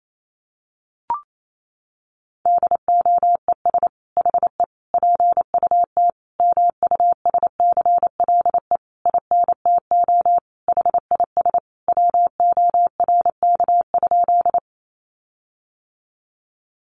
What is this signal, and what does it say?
Morse code, heard as a keyed tone, about 28 wpm, 700 Hz. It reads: DOES HE PUT MUSCLE INTO HIS WORK?
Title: Sentences from Top 1300 Words 28wpm